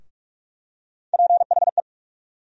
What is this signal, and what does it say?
Morse code, heard as a keyed tone, about 44 wpm, 700 Hz. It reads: PHE
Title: USA Call Sign Postfixes - 3 Letters 45wpm